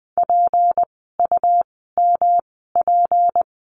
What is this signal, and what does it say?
Morse code, heard as a keyed tone, about 20 wpm, spaced as for 15 wpm, 700 Hz. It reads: PUMP
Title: 153 Lesson 25f - Letter p - Words 20wpm